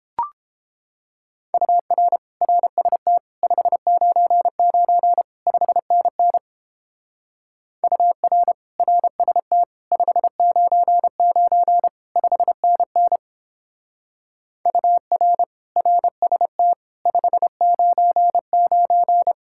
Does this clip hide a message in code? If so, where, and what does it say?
Yes: Morse code from 1.5 to 19.3 s, heard as a keyed tone, about 29 wpm, 700 Hz. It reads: UR RST 599 5NN UR RST 599 5NN UR RST 599